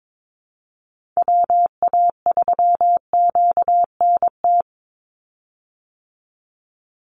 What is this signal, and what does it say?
Morse code, heard as a keyed tone, about 22 wpm, 700 Hz. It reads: WA3QNT
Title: Call Signs Repeated 3x - 22wpm